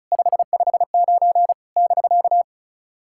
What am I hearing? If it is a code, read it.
Morse code, heard as a keyed tone, about 36 wpm, 690 Hz. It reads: 559 <BK>
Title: SOTA QSO Practice 35wpm